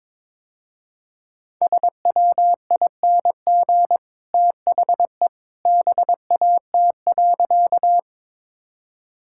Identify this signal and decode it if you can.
Morse code, heard as a keyed tone, about 22 wpm, 700 Hz. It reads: SWING THE BAT.
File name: practice / 22wpm / Easy Sentences with Hints - 22wpm